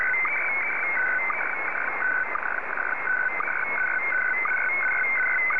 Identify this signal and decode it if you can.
SSTV (slow-scan television), Scottie DX.